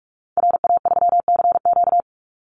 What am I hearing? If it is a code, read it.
Morse code, heard as a keyed tone, about 44 wpm, 700 Hz. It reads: RA3CQ
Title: Call Signs Repeated 2x - 45wpm